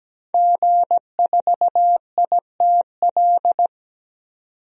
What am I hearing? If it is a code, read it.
Morse code, heard as a keyed tone, about 17 wpm, 700 Hz. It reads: G4ITL
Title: DXCC-100 Call Signs - 5 Characters - 17wpm